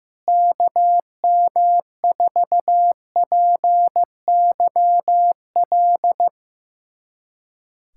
Morse code, heard as a keyed tone, about 16 wpm, 700 Hz. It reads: KM4PYL
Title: USA Callsign DE Callsign 15wpm